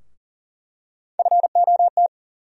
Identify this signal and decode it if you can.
Morse code, heard as a keyed tone, about 40 wpm, 700 Hz. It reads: FOT
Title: USA Call Sign Postfixes - 3 Letters 40wpm